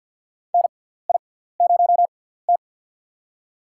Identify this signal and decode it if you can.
Morse code, heard as a keyed tone, about 52 wpm, spaced as for 17 wpm, 700 Hz. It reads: NI0T
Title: USA Call Signs - 4 Characters - 6x Character Spacing - 50wpm